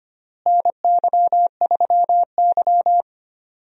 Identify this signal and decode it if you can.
Morse code, heard as a keyed tone, about 25 wpm, 700 Hz. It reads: NY3Y
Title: USA Call Signs - 4 Characters - Repeated 3x - 25wpm